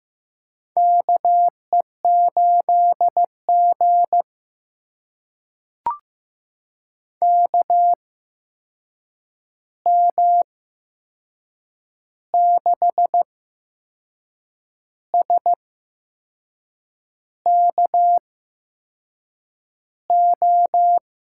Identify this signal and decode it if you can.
Morse code, heard as a keyed tone, about 15 wpm, spaced as for 6 wpm, 700 Hz. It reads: KE8G KM6SKO
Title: USA Call Signs - 8x Character Spacing - 15wpm